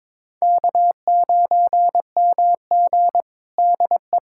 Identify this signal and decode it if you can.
Morse code, heard as a keyed tone, about 22 wpm, 700 Hz. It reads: K9MG DE
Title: USA Callsign DE Callsign - 4 Characters - 22wpm